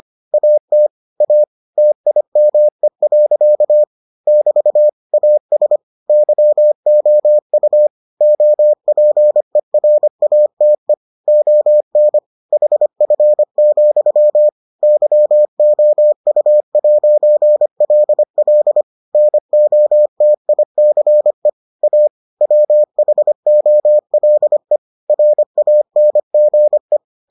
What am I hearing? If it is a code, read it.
Morse code, heard as a keyed tone, about 25 wpm, 600 Hz. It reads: AT A TIME. <BT> AS YOU OPERATE ON HF, YOU'LL NOTICE A WHOLE RANGE